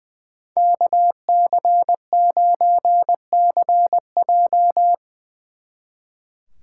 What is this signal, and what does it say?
Morse code, heard as a keyed tone, about 22 wpm, 700 Hz. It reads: KC9CJ
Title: Speed Racing - USA 2x2 Call Signs 20wpm